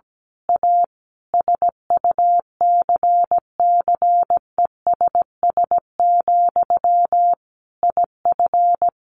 Morse code, heard as a keyed tone, about 17 wpm, 700 Hz. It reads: A SUCCESS, IF
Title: A Sound Of Thunder 17wpm